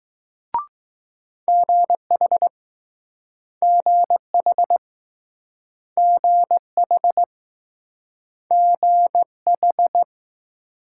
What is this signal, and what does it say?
Morse code, heard as a keyed tone, about 20 wpm, 700 Hz. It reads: GH GH GH GH